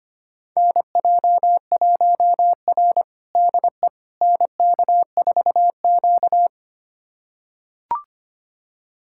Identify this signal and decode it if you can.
Morse code, heard as a keyed tone, about 25 wpm, 700 Hz. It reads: NJ1R DE NK4Q
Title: USA Callsign DE Callsign - 4 Characters - 25wpm